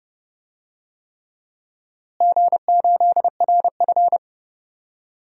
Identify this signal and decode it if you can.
Morse code, heard as a keyed tone, about 30 wpm, 700 Hz. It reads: G8RF
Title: DXCC-100 Call Signs - 4 Characters - Repeated 3x - 30wpm